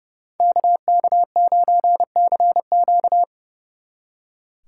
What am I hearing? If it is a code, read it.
Morse code, heard as a keyed tone, about 38 wpm, 700 Hz. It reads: KK9CQ
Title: Speed Racing - USA 2x2 Call Signs 30wpm